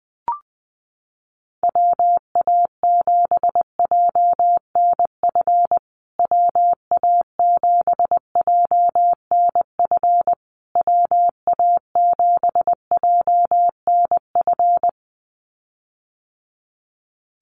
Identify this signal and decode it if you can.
Morse code, heard as a keyed tone, about 20 wpm, 700 Hz. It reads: WA7JNF WA7JNF WA7JNF
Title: USA Call Signs Repeated 3x 20wpm